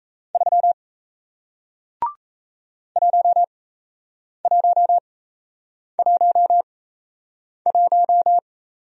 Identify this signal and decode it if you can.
Morse code, heard as a keyed tone, about 38 wpm, spaced as for 8 wpm, 700 Hz. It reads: 3 1111